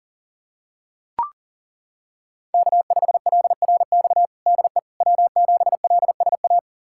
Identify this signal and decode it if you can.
Morse code, heard as a keyed tone, about 40 wpm, 700 Hz. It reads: K5LRX DE W7LSA
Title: USA Callsign DE Callsign 40wpm